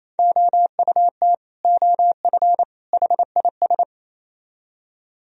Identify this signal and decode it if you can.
Morse code, heard as a keyed tone, about 33 wpm, 700 Hz. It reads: OUT OF HIS
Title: Speed-Racing - Sets of 3 Words 28wpm